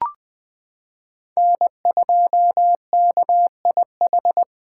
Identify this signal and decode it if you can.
Morse code, heard as a keyed tone, about 20 wpm, 700 Hz. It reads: N2KIH
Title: USA Callsign DE Callsign - 5 Characters - 20wpm